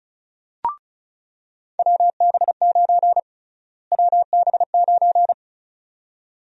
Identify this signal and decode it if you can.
Morse code, heard as a keyed tone, about 36 wpm, 700 Hz. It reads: WB9 WB9
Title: USA Call Sign Prefixes - Repeated 2x - 35wpm